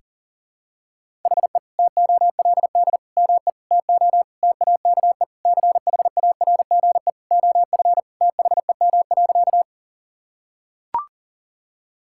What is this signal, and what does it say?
Morse code, heard as a keyed tone, about 40 wpm, 700 Hz. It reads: HE TOLD ME TO TAKE CHARGE OF THEM.
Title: Sentences from Top 1000 Words - Review Part 6 - 40wpm